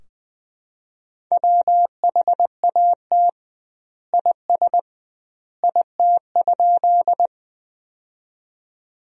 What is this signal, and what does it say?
Morse code, heard as a keyed tone, about 20 wpm, 700 Hz. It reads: WHAT IS IT?
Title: 2x Word-Spacing - Sentences from Top 100 Words - 20wpm